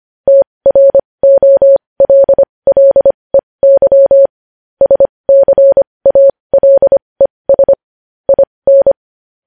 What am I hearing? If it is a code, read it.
Morse code, heard as a keyed tone, about 25 wpm, spaced as for 20 wpm, 550 Hz. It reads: TROLLEY SCALES IN
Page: news-headlines-in-morse-code-at-20-wpm